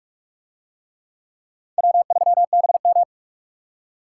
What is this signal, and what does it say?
Morse code, heard as a keyed tone, about 42 wpm, 700 Hz. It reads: W3BM